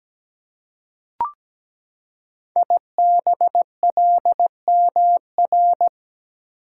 Morse code, heard as a keyed tone, about 17 wpm, 700 Hz. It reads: IBLMR
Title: Code Groups - 5 Letters - 17wpm